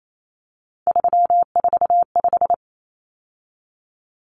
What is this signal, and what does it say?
Morse code, heard as a keyed tone, about 28 wpm, 700 Hz. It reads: 345
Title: Numbers - 3-Digit 28wpm